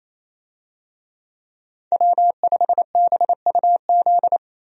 Morse code, heard as a keyed tone, about 28 wpm, 700 Hz. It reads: W5BUZ
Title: USA Call Signs - 5 Characters - Repeated 2x - 28wpm